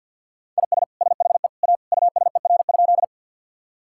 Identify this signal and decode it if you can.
Morse code, heard as a keyed tone, about 52 wpm, 700 Hz. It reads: IS SHE A USER?
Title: Sentences from Top 1000 Words - Review Part 5 - 50wpm